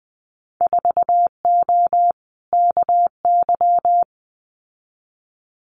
Morse code, heard as a keyed tone, about 20 wpm, 700 Hz. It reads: 4O KY
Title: Winter Field Day Exchanges Encore 20wpm